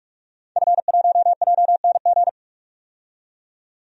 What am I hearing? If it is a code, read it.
Morse code, heard as a keyed tone, about 47 wpm, 700 Hz. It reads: F1JNG
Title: DXCC-100 Call Signs - 5 Characters - 45wpm